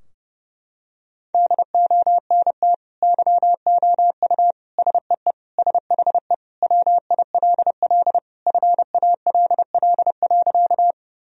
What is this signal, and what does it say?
Morse code, heard as a keyed tone, about 32 wpm, 700 Hz. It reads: DONT YOU SEE SHE WILL FALL.
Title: Sentences from Top 1000 Words - Review Part 2 - 30wpm